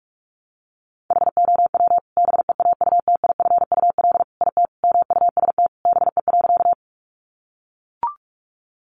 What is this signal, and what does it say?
Morse code, heard as a keyed tone, about 44 wpm, 700 Hz. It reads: HOW BEAUTIFUL IT MUST BE.
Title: Sentences from Top 900 Words 45wpm